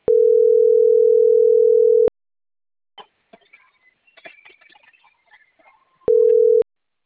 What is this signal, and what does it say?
Ringback tone (North America), 440 + 480 Hz.